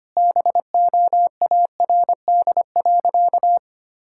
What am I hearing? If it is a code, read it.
Morse code, heard as a keyed tone, about 25 wpm, 700 Hz. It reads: BOARD.
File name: practice / 25wpm / Sentences with 3-Word Phrases - 25wpm